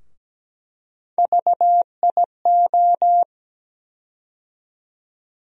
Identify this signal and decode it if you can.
Morse code, heard as a keyed tone, about 17 wpm, 700 Hz. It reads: VIO
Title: USA Call Sign Postfixes - 3 Letters 17wpm